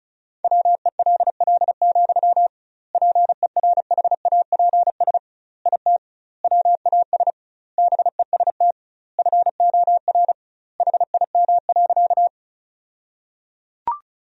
Morse code, heard as a keyed tone, about 36 wpm, 700 Hz. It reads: WELL, PERHAPS IT WAS BEST FOR HIM.
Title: 2x Word-Spacing - Sentences from Top 500 Words - 35wpm